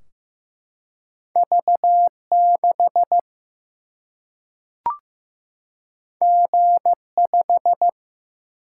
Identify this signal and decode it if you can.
Morse code, heard as a keyed tone, about 15 wpm, 700 Hz. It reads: V6 G5
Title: Single-Letter Single-Number 15wpm